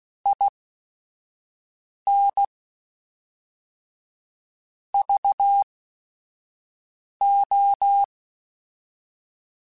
Morse code, heard as a keyed tone, about 16 wpm, spaced as for 5 wpm, 800 Hz. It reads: IN VO
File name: Technical Subjects / Technical Subjects-mp3s / Reactance @5